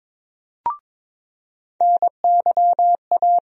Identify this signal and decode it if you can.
Morse code, heard as a keyed tone, about 22 wpm, 700 Hz. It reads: NYA
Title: USA Call Sign Postfixes - 3 Letters 22wpm